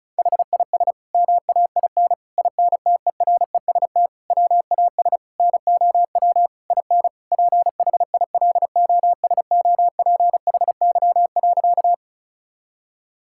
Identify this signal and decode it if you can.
Morse code, heard as a keyed tone, about 34 wpm, 700 Hz. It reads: HIS MAIN INTEREST WAS NOW IN PHILOSOPHY.
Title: Sentences from Top 1000 Words - Review Part 6 - 35wpm